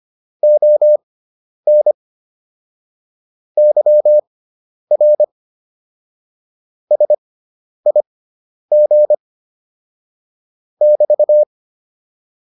Morse code, heard as a keyed tone, about 25 wpm, spaced as for 10 wpm, 600 Hz. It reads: ON YR SIG <BT>
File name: QSO407 10